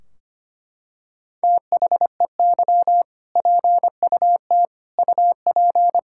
Morse code, heard as a keyed tone, about 25 wpm, 700 Hz. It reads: THEY PUT UP
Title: Sentences from Top 1600 Words 25wpm